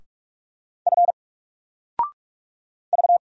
Morse code, heard as a keyed tone, about 44 wpm, 700 Hz. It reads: F V
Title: Single Character (Letter:Number:Punctuation:Prosign) 45wpm